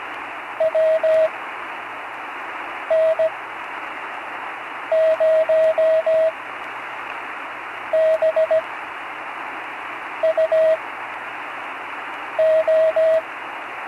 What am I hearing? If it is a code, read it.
Morse code, heard as a keyed tone, about 17 wpm, spaced as for 5 wpm, 640 Hz, with radio noise. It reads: WN0BUO
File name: CW Learning Programs / Learning Morse Code (from Tune in the World) / 35 answers